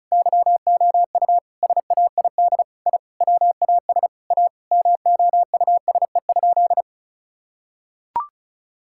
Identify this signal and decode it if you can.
Morse code, heard as a keyed tone, about 34 wpm, 700 Hz. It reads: YOU SAID I WAS A MOUSE?
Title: Sentences from Top 1000 Words - Review Part 5 - 35wpm